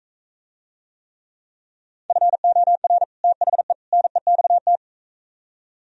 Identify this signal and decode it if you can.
Morse code, heard as a keyed tone, about 42 wpm, 700 Hz. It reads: FOR THE NEXT